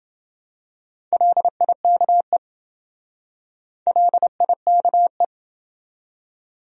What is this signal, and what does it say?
Morse code, heard as a keyed tone, about 27 wpm, 700 Hz. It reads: LIKE LIKE